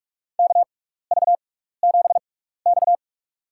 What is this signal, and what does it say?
Morse code, heard as a keyed tone, about 42 wpm, spaced as for 15 wpm, 700 Hz. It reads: KV7X
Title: USA Call Signs - 4 Characters - 6x Character Spacing - 45wpm